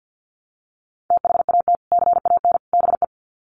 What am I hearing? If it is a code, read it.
Morse code, heard as a keyed tone, about 50 wpm, 700 Hz. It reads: THAT CAN BE
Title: Sets of 3 Words 50wpm